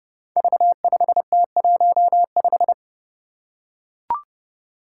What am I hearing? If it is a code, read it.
Morse code, heard as a keyed tone, about 28 wpm, 700 Hz. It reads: V5T15
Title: Five Challenging Letter-Number Combinations 30wpm